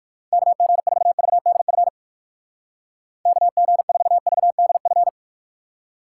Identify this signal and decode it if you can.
Morse code, heard as a keyed tone, about 59 wpm, 700 Hz. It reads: KG4VDF KG4VDF